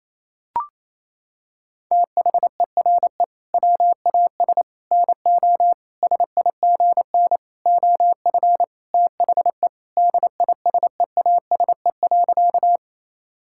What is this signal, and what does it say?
Morse code, heard as a keyed tone, about 28 wpm, 700 Hz. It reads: THERE WAS NO SIGN OF THE DISEASE.
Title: Sentences from Top 1000 Words - Review Part 6 - 28wpm